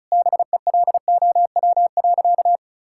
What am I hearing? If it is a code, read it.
Morse code, heard as a keyed tone, about 36 wpm, 700 Hz. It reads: BELOW.
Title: Sentences from Top 1000 Words - Review Part 5 - 35wpm